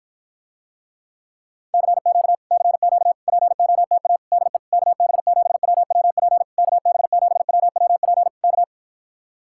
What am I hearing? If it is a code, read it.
Morse code, heard as a keyed tone, about 53 wpm, 700 Hz. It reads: CQ CQ POTA DE KB7WWP KB7WWP K